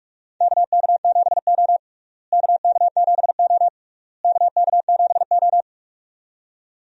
Morse code, heard as a keyed tone, about 41 wpm, 700 Hz. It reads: KK7O KK7O KK7O
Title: USA Call Signs - 4 Characters - Repeated 3x - 45wpm